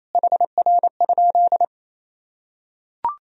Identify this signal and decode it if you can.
Morse code, heard as a keyed tone, about 28 wpm, 700 Hz. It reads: HR?
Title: Sentences from Top 400 Words and Abbreviations 28wpm